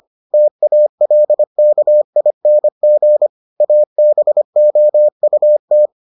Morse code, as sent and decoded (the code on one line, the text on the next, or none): - .- .-.. -.- .. -. --. / .- -... --- ..- -
TALKING ABOUT